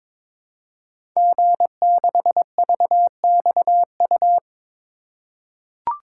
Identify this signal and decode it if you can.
Morse code, heard as a keyed tone, about 22 wpm, 700 Hz. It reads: G6VXU
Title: Code Groups - 5 Letter:Numbers - 22wpm